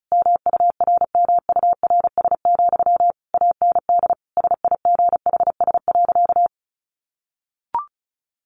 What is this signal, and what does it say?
Morse code, heard as a keyed tone, about 34 wpm, 700 Hz. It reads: MURMURS, AND SIGHS.
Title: A Sound Of Thunder 35wpm